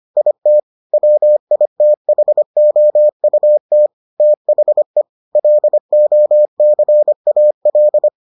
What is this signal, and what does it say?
Morse code, heard as a keyed tone, about 25 wpm, 600 Hz. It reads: IT WITHOUT THE LOCAL